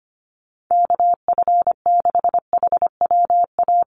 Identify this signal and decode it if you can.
Morse code, heard as a keyed tone, about 25 wpm, 700 Hz. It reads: KF6HWA
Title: USA Call Signs Repeated 2x 25wpm